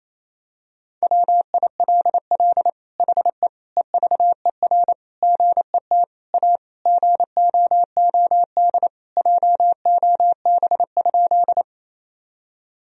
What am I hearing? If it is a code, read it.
Morse code, heard as a keyed tone, about 28 wpm, 700 Hz. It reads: WILL HE EVER GET A GOOD JOB?
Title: Sentences from Top 1000 Words - Review Part 3 - 28wpm